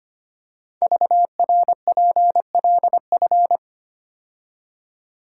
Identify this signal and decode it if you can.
Morse code, heard as a keyed tone, about 25 wpm, 700 Hz. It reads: VRPLF
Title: Code Groups - 5 Letters - 25wpm